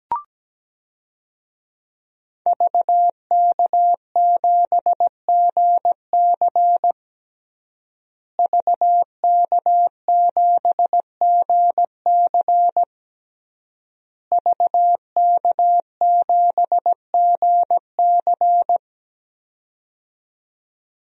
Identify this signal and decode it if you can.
Morse code, heard as a keyed tone, about 17 wpm, 700 Hz. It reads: VK7GC VK7GC VK7GC
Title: DXCC-100 Call Signs - 5 Characters - Repeated 3x - 17wpm